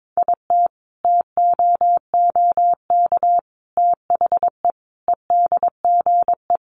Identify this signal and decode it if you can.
Morse code, heard as a keyed tone, about 22 wpm, 700 Hz. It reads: IT TOOK THE EDGE
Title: Sentences from Top 1300 Words 22wpm